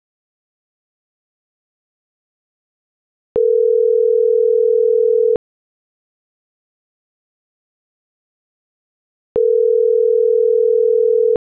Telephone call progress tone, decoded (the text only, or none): ringback tone